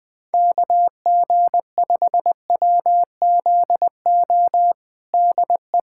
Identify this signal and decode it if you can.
Morse code, heard as a keyed tone, about 20 wpm, 700 Hz. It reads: KG5WZO DE
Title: USA Callsign DE Callsign 20wpm